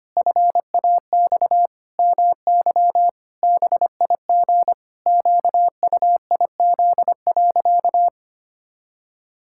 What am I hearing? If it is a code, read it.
Morse code, heard as a keyed tone, about 25 wpm, 700 Hz. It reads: FAX MY BIG QUIZ.
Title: Pangram Challenge v2 - 25wpm